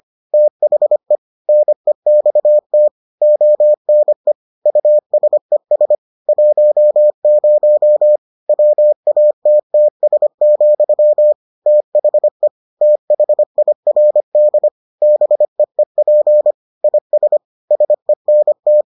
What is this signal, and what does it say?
Morse code, heard as a keyed tone, about 25 wpm, 600 Hz. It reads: THE NEXT ONE USES 10 WATTS, THE THIRD BEEP IS SENT